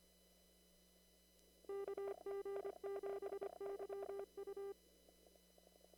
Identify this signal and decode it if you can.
Morse code, heard as a keyed tone, about 25 wpm, 400 Hz. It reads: KG7YU
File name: CW / kt5j wpx cw-op-k5pi